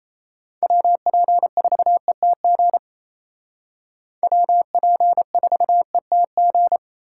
Morse code, heard as a keyed tone, about 36 wpm, 700 Hz. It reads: WP4ETG WP4ETG